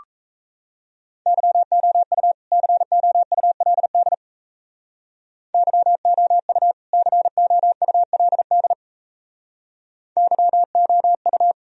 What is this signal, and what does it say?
Morse code, heard as a keyed tone, about 37 wpm, 700 Hz. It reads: YOU COULD YOU COULD YOU